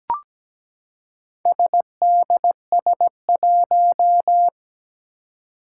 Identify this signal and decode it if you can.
Morse code, heard as a keyed tone, about 17 wpm, 700 Hz. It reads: SDS1
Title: Four Challenging Letter-Number Combinations 17wpm